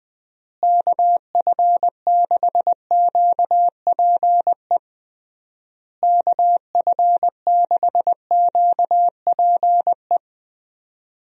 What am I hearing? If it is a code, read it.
Morse code, heard as a keyed tone, about 20 wpm, 700 Hz. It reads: KF6QPE KF6QPE
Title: USA Call Signs - 6 Characters - Repeated 3x - 20wpm